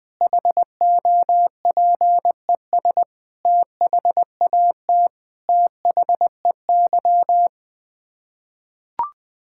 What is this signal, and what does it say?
Morse code, heard as a keyed tone, about 20 wpm, 700 Hz. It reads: HOPES THAT THEY
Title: Sets of 9 Words - 20wpm